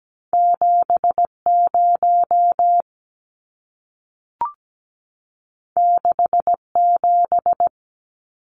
Morse code, heard as a keyed tone, about 17 wpm, 700 Hz. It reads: 70 67
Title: Numbers - 2-Digit 17wpm